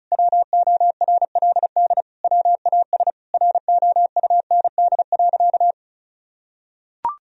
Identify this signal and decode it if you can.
Morse code, heard as a keyed tone, about 34 wpm, 700 Hz. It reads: WORLD WAS ROUND.
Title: Sentences from Top 1000 Words - Review Part 1 - 35wpm